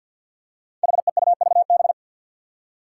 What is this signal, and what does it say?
Morse code, heard as a keyed tone, about 52 wpm, 700 Hz. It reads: HEUVB
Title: Five Challenging Letter-Number Combinations 50wpm